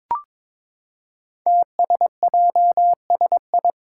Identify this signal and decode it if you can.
Morse code, heard as a keyed tone, about 22 wpm, 700 Hz. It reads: TSJSI
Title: Five Challenging Letter-Number Combinations 22wpm